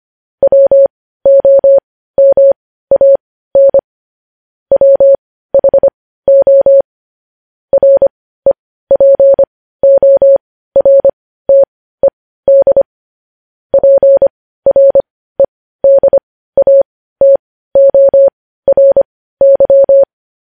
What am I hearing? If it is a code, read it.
Morse code, heard as a keyed tone, about 25 wpm, spaced as for 15 wpm, 550 Hz. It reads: WOMAN WHO REPORTED PREDATORY
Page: news headlines in morse code at 15 wpm